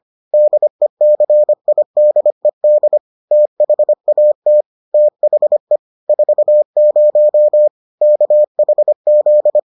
Morse code, heard as a keyed tone, about 25 wpm, 600 Hz. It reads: DECIDED THAT THE 40 KHZ